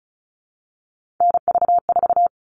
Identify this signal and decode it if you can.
Morse code, heard as a keyed tone, about 34 wpm, 700 Hz. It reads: NV4
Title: All Cut Numbers - 3-Digit 35wpm